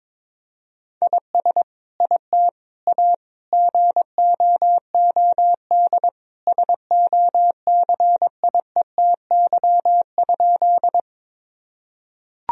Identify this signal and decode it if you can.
Morse code, heard as a keyed tone, about 22 wpm, 700 Hz. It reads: IS IT A GOOD SOCIETY?
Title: Sentences from Top 1000 Words - Review Part 4 - 22wpm